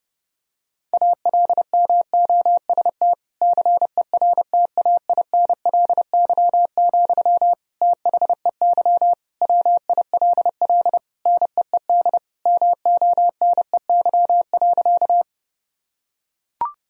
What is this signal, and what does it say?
Morse code, heard as a keyed tone, about 31 wpm, 700 Hz. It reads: ALMOST CERTAINLY, THEY WILL NEED MONEY.
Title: Sentences from Top 1000 Words - Review Part 4 - 30wpm